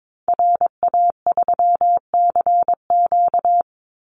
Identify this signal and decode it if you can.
Morse code, heard as a keyed tone, about 22 wpm, 700 Hz. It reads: RA3CQ
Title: Call Signs Repeated 3x - 22wpm